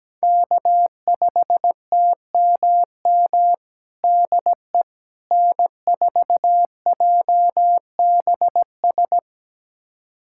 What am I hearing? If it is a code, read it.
Morse code, heard as a keyed tone, about 17 wpm, 700 Hz. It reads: K5TMM DE N4JBS
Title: USA Callsign DE Callsign - 5 Characters - 17wpm